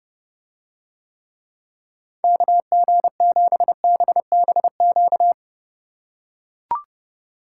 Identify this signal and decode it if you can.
Morse code, heard as a keyed tone, about 32 wpm, 700 Hz. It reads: KG7BBQ